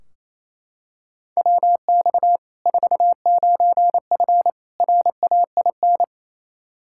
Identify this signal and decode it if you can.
Morse code, heard as a keyed tone, about 28 wpm, 700 Hz. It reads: WX 49F RAIN
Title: Practice QSO - Exchange 2 - Info - 28wpm